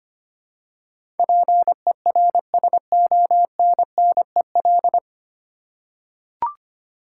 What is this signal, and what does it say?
Morse code, heard as a keyed tone, about 25 wpm, 700 Hz. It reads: PERSONNEL
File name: practice / 25wpm / 9 Letter Words - First 3 Letters Given - 25wpm